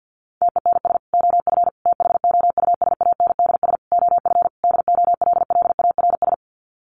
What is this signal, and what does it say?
Morse code, heard as a keyed tone, about 50 wpm, 700 Hz. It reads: TENS OF THOUSANDS OF DOLLARS
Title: Sets of 5 Words Encore2 50wpm